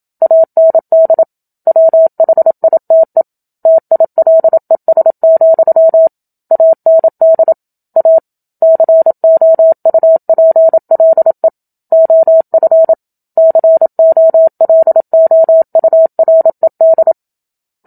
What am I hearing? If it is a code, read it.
Morse code, heard as a keyed tone, about 27 wpm, 650 Hz. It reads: AND WHITE TILES, AND A COUPLE OF COLOURED
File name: Book Two / 01-Chapter 1 - 27 WPM